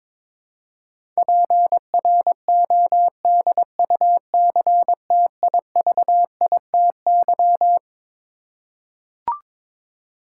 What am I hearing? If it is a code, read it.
Morse code, heard as a keyed tone, about 22 wpm, 700 Hz. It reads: PRODUCTIVITY
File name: Twelve Letter Words - 22wpm